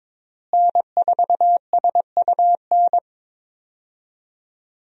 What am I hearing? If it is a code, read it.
Morse code, heard as a keyed tone, about 22 wpm, 700 Hz. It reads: N4SUN
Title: USA Call Signs - 5 Characters - Repeated 3x - 22wpm